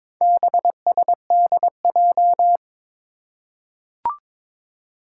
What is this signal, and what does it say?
Morse code, heard as a keyed tone, about 22 wpm, 700 Hz. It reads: BSDJ
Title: Four Challenging Letter-Number Combinations 22wpm